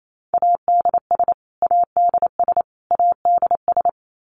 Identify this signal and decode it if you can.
Morse code, heard as a keyed tone, about 28 wpm, 700 Hz. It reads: ADS ADS ADS
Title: Abbreviations Repeated 3x - 28wpm